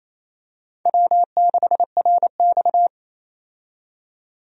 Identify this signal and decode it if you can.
Morse code, heard as a keyed tone, about 28 wpm, 700 Hz. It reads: W6RX
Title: USA Call Signs - 4 Characters - 28wpm